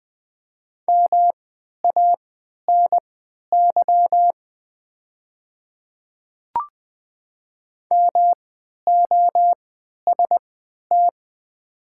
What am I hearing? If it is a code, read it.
Morse code, heard as a keyed tone, about 20 wpm, spaced as for 11 wpm, 700 Hz. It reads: MANY MOST
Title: 3x Character Spacing - Top 100 Words - 20wpm